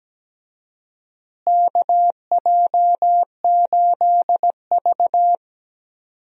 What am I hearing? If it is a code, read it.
Morse code, heard as a keyed tone, about 17 wpm, 700 Hz. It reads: KJ8V
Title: USA Call Signs - 4 Characters - Repeated 3x - 17wpm